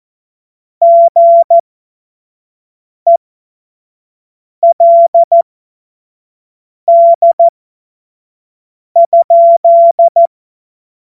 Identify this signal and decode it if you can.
Morse code, heard as a keyed tone, about 14 wpm, spaced as for 5 wpm, 680 Hz. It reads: GELD?
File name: Morsekurs / Witze / Witz 25. Januar 2022, 25 F70